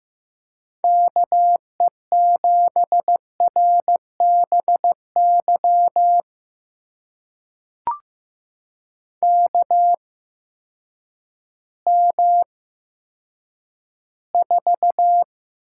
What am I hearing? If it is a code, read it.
Morse code, heard as a keyed tone, about 15 wpm, spaced as for 9 wpm, 700 Hz. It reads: KE7RBY KM4
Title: USA Call Signs - 6 Characters - 8x Character Spacing - 15wpm